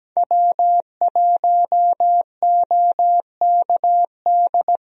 Morse code, heard as a keyed tone, about 17 wpm, 700 Hz. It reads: W1OKD
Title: USA Call Signs - 5 Characters - 17wpm